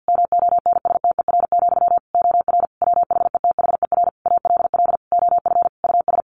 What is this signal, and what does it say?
Morse code, heard as a keyed tone, about 50 wpm, 700 Hz. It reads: MONSTER, OR WHETHER ALL OF US
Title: A Sound Of Thunder 50wpm